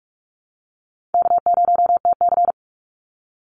Morse code, heard as a keyed tone, about 44 wpm, 700 Hz. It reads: K0TC
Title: Easy Call Signs Repeated 3x - 45wpm